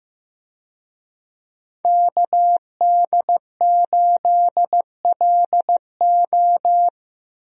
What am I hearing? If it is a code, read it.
Morse code, heard as a keyed tone, about 15 wpm, 700 Hz. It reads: KD8LO
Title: USA Call Signs - 5 Characters - Repeated 3x - 15wpm